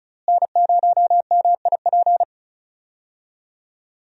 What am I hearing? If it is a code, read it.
Morse code, heard as a keyed tone, about 36 wpm, 700 Hz. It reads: N0MIP
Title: USA Call Signs - 5 Characters - Repeated 2x - 35wpm